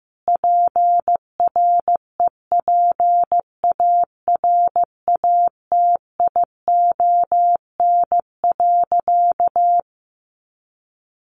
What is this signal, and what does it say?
Morse code, heard as a keyed tone, about 15 wpm, 700 Hz. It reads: PREPARATION.
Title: Sentences from Top 1400 Words 15wpm